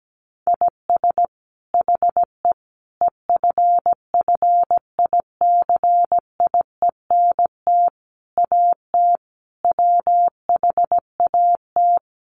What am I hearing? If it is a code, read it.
Morse code, heard as a keyed tone, about 17 wpm, 700 Hz. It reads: IS HE EFFICIENT AT WHAT
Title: Sentences from Top 1300 Words 17wpm